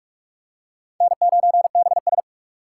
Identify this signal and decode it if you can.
Morse code, heard as a keyed tone, about 40 wpm, 700 Hz. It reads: N9BS